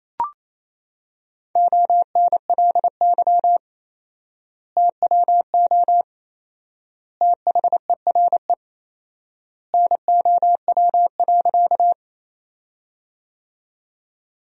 Morse code, heard as a keyed tone, about 27 wpm, 700 Hz. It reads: ONLY TWO THERE NOW.
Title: 4x Word-Spacing - Sentences from Top 200 Words - 28wpm